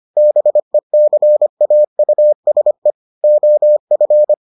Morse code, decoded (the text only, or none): BECAUSE OF